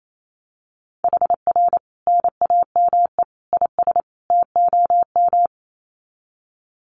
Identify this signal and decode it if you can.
Morse code, heard as a keyed tone, about 28 wpm, 700 Hz. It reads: HR NAME IS TOM
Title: Common QSO Phrases 28wpm